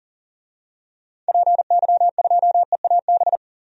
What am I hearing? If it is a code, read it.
Morse code, heard as a keyed tone, about 38 wpm, 700 Hz. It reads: PY2EAB
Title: DXCC-100 Call Signs - 6 Characters - 40wpm